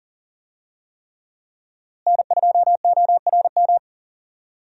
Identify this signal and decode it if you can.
Morse code, heard as a keyed tone, about 40 wpm, 700 Hz. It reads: N2ORM
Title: USA Call Signs - 5 Characters - Repeated 3x - 40wpm